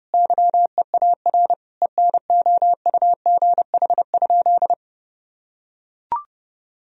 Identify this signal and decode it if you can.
Morse code, heard as a keyed tone, about 32 wpm, 700 Hz. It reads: YEAR ENOUGH?
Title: Sentences from Top 1000 Words - Review Part 2 - 30wpm